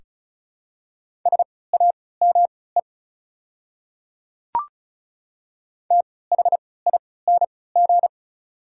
Morse code, heard as a keyed tone, about 36 wpm, spaced as for 20 wpm, 700 Hz. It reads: SAME THING
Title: 3x Character Spacing - Top 200 Words - 35wpm